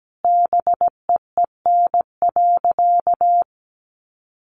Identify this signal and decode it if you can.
Morse code, heard as a keyed tone, about 17 wpm, 700 Hz. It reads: BEEN.
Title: Sentences from Top 300 Words 17wpm